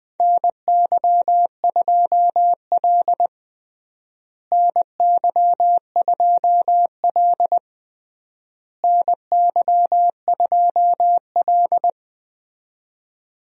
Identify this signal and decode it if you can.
Morse code, heard as a keyed tone, about 20 wpm, 700 Hz. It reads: NY2L NY2L NY2L
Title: USA Call Signs - 4 Characters - Repeated 3x - 20wpm